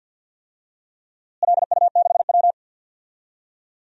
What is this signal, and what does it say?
Morse code, heard as a keyed tone, about 51 wpm, 700 Hz. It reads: LU6W
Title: DXCC-100 Call Signs - 4 Characters - Repeated 3x - 50wpm